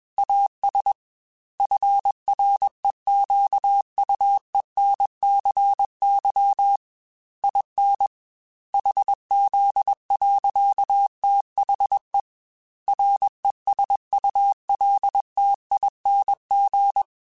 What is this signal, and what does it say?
Morse code, heard as a keyed tone, about 21 wpm, 800 Hz. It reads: AS FREQUENCY IN HZ. THE RESULTING
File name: Technical Subjects / Technical Subjects-mp3s / XC @20